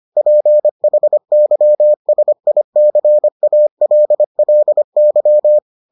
Morse code, heard as a keyed tone, about 25 wpm, 600 Hz. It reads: PHYSICALLY